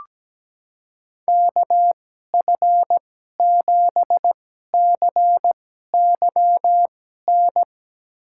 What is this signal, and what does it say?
Morse code, heard as a keyed tone, about 17 wpm, spaced as for 12 wpm, 700 Hz. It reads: KF7CYN
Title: USA Call Signs - 6 Characters - 2x Character Spacing - 17wpm